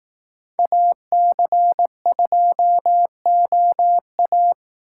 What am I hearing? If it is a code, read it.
Morse code, heard as a keyed tone, about 18 wpm, 700 Hz. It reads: AC2OA
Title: Speed Racing - USA Call Signs 15wpm